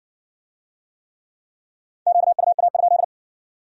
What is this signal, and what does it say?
Morse code, heard as a keyed tone, about 57 wpm, 700 Hz. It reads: QUA?